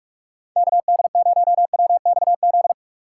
Morse code, heard as a keyed tone, about 44 wpm, 700 Hz. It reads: KD0WXZ
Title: USA Call Signs - 6 Characters - 6x Character Spacing - 45wpm